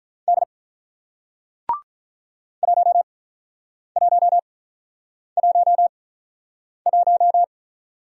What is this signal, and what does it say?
Morse code, heard as a keyed tone, about 48 wpm, spaced as for 9 wpm, 700 Hz. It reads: D 1111